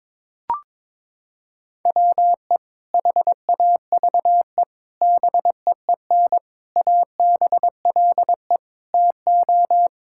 Morse code, heard as a keyed tone, about 22 wpm, 700 Hz. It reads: WE HAVE BEEN ABLE TO
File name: practice / 22wpm / Sentences with 4-Word Phrases - 22wpm